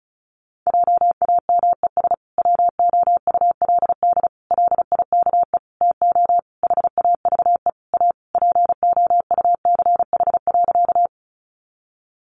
Morse code, heard as a keyed tone, about 34 wpm, 700 Hz. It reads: JAMES WOULD LIKE TO HAVE A POUCH.
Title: Easy Phrases 35wpm